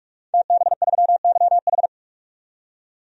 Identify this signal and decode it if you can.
Morse code, heard as a keyed tone, about 44 wpm, 700 Hz. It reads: TB3YH
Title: DXCC-100 Call Signs - 5 Characters - 45wpm